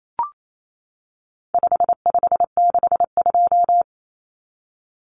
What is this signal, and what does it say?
Morse code, heard as a keyed tone, about 28 wpm, 700 Hz. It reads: 5562
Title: Common Cut Numbers - 4-Digit 28wpm